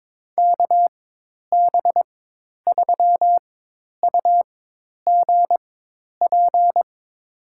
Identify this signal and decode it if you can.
Morse code, heard as a keyed tone, about 22 wpm, spaced as for 10 wpm, 700 Hz. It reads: KB3UGP
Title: USA Call Signs - 6 Characters - 4x Character Spacing - 22wpm